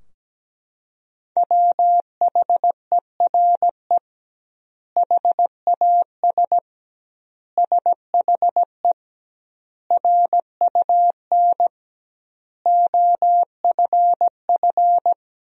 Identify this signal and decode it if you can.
Morse code, heard as a keyed tone, about 17 wpm, 700 Hz. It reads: WHERE HAS SHE RUN OFF
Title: 2x Word-Spacing - Sentences from Top 300 Words - 17wpm